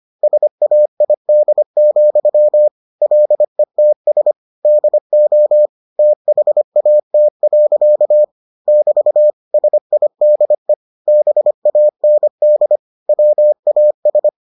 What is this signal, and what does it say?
Morse code, heard as a keyed tone, about 25 wpm, 600 Hz. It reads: SAID, LETS DO THAT. <BT> SIDE BAND WAS